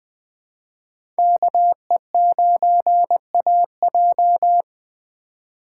Morse code, heard as a keyed tone, about 21 wpm, 700 Hz. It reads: KE9AJ
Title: 2023 SOTA Chaser Call Signs - 20wpm